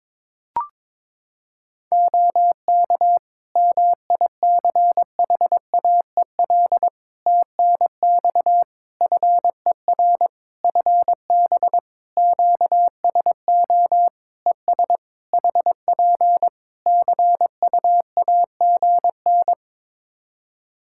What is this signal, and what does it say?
Morse code, heard as a keyed tone, about 22 wpm, 700 Hz. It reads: OK MICHAEL TNX FER FB QSO ES HP CUAGN
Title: K1USN SST QSO Practice 22wpm